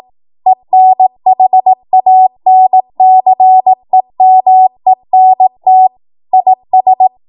Morse code, heard as a keyed tone, about 18 wpm, 750 Hz. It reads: ENHANCEMENT IS
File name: Morse / 18WPM